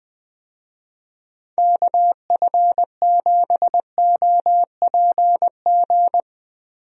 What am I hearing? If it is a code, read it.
Morse code, heard as a keyed tone, about 20 wpm, 700 Hz. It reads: KF7OPG
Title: USA Call Signs - 6 Characters - Repeated 3x - 20wpm